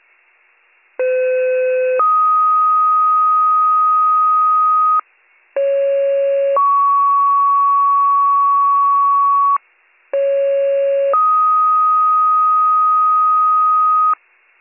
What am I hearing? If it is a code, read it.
Two-tone page, heard as a one-second tone, then a three-second tone, about 510 then 1250 Hz, about 550 then 1120 Hz, about 550 then 1250 Hz. It is heard from 1.0 to 14.1 s.